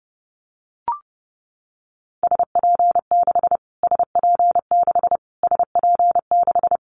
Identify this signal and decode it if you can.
Morse code, heard as a keyed tone, about 30 wpm, 700 Hz. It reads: SP6 SP6 SP6
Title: Callsign Prefixes for Top 25 DXCC - Repeated 3x - 30wpm